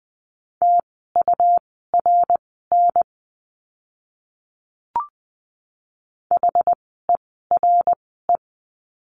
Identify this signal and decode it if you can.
Morse code, heard as a keyed tone, about 20 wpm, spaced as for 14 wpm, 700 Hz. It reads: TURN HERE
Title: 230 Lesson 37f - Letter x - Words 20wpm